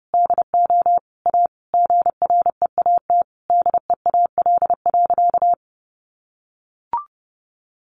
Morse code, heard as a keyed tone, about 30 wpm, 700 Hz. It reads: DO A GREAT DEAL.
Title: Sentences from Top 1300 Words 30wpm